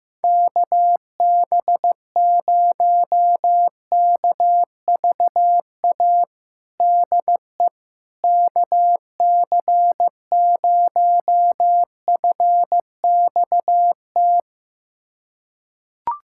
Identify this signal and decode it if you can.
Morse code, heard as a keyed tone, about 15 wpm, 700 Hz. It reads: KB0KVA DE KC0FXT
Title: USA Callsign DE Callsign - 6 Characters - 15wpm